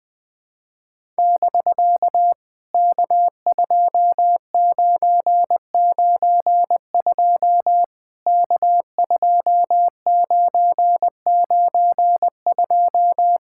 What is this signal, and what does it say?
Morse code, heard as a keyed tone, about 18 wpm, 700 Hz. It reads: <BK> K2992 K2992
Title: POTA QSO Practice - 20wpm